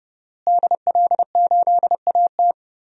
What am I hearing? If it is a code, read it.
Morse code, heard as a keyed tone, about 31 wpm, 700 Hz. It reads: DL8AT
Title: DXCC-100 Call Signs - 5 Characters - Repeated 3x - 30wpm